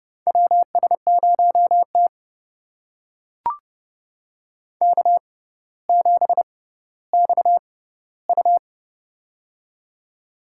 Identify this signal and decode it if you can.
Morse code, heard as a keyed tone, about 29 wpm, spaced as for 15 wpm, 700 Hz. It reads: WS0T K7XU
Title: USA Call Signs - 4 Characters - 6x Character Spacing - 30wpm